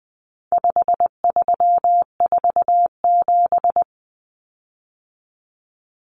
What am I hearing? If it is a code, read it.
Morse code, heard as a keyed tone, about 20 wpm, 700 Hz. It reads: 5347
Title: Common Cut Numbers - 4-Digit 20wpm